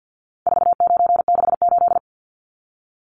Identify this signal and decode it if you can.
Morse code, heard as a keyed tone, about 50 wpm, 700 Hz. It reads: 4968
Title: Numbers - 4-Digit 50wpm